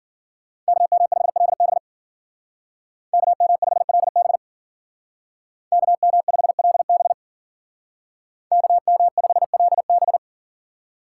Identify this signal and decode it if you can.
Morse code, heard as a keyed tone, about 50 wpm, 700 Hz. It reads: KM5LB KM5LB KM5LB KM5LB